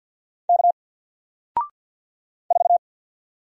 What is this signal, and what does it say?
Morse code, heard as a keyed tone, about 48 wpm, 700 Hz. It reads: K 4
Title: Single Letter-Number Alt 50wpm